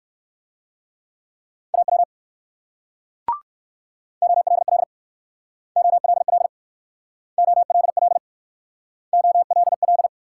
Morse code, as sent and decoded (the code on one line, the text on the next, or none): .- .-- / --- .-.. .-.. / --- .-.. .-.. / --- .-.. .-.. / --- .-.. .-..
AW OLL OLL OLL OLL